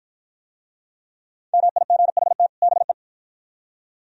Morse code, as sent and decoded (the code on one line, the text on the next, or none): -- .. --. .... - / -... .
MIGHT BE